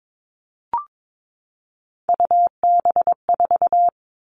Morse code, heard as a keyed tone, about 22 wpm, 700 Hz. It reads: UB4
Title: All Cut Numbers - 3-Digit 22wpm